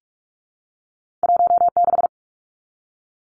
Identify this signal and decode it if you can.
Morse code, heard as a keyed tone, about 44 wpm, 700 Hz. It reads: 16
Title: Numbers - 2-Digit 45wpm